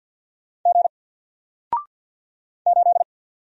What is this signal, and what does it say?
Morse code, heard as a keyed tone, about 50 wpm, 700 Hz. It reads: G 8
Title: Single Letter-Number 50wpm